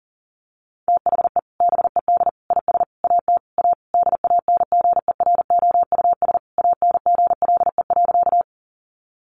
Morse code, heard as a keyed tone, about 40 wpm, 700 Hz. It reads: THE BED IS AT A DANGEROUS ANGLE.
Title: Sentences from Top 1500 Words 40wpm